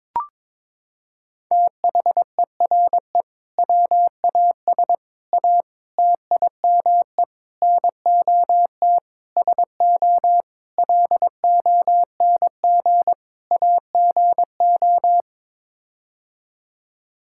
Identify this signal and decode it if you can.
Morse code, heard as a keyed tone, about 21 wpm, 700 Hz. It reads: THERE WAS A TIME NOT SO LONG AGO
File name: practice / 22wpm / Sets of 8 Words - 22wpm